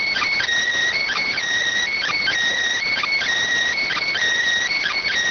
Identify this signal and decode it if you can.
Emergency Alert System (SAME) tones, heard as harsh buzzing bursts. It is heard from 0.9 to 5.1 s.